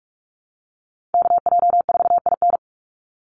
Morse code, heard as a keyed tone, about 44 wpm, 700 Hz. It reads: KJ4IN
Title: USA Call Signs 45wpm